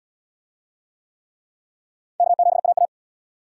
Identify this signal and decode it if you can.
Morse code, heard as a keyed tone, about 76 wpm, 700 Hz. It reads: K7AA